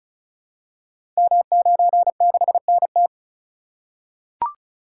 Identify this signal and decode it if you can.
Morse code, heard as a keyed tone, about 36 wpm, 700 Hz. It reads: M96NT
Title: Code Groups - 5 Letter:Numbers - 35wpm